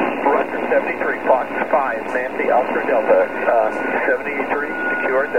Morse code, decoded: none